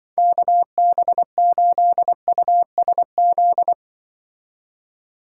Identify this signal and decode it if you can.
Morse code, heard as a keyed tone, about 24 wpm, 700 Hz. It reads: KB8USZ
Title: Speed Racing - USA Call Signs 20wpm